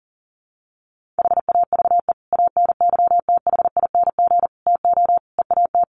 Morse code, heard as a keyed tone, about 40 wpm, 700 Hz. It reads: HAVE ANYTHING TO EAT
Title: Sentences from Top 1500 Words 40wpm